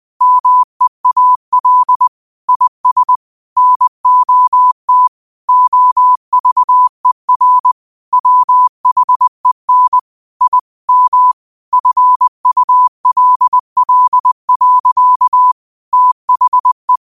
Morse code, heard as a keyed tone, about 20 wpm, 1000 Hz. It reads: MEAL IS NOT OVER WHEN I M FULL. THE